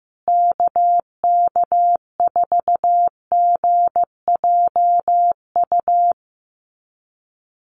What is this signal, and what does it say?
Morse code, heard as a keyed tone, about 15 wpm, 700 Hz. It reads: KK4GJU
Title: USA Call Signs Repeated 3x 15wpm